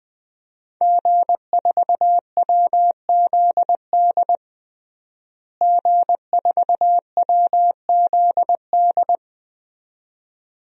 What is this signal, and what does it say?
Morse code, heard as a keyed tone, about 20 wpm, 700 Hz. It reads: G4WZD G4WZD
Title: DXCC-100 Call Signs - 5 Characters - Repeated 2x - 20wpm